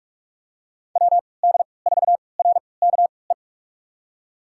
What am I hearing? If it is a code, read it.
Morse code, heard as a keyed tone, about 45 wpm, spaced as for 25 wpm, 700 Hz. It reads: WD4RKE
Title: USA Call Signs - 6 Characters - 3x Character Spacing - 45wpm